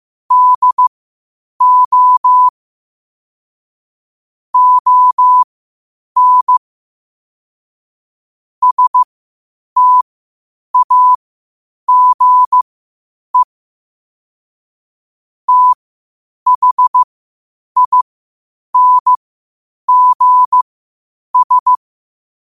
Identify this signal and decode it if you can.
Morse code, heard as a keyed tone, about 15 wpm, spaced as for 8 wpm, 1000 Hz. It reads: DO ON STAGE THINGS
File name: qotdfiles / 8wpm